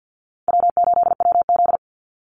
Morse code, heard as a keyed tone, about 50 wpm, 700 Hz. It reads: W8WZ
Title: Call Signs - 50wpm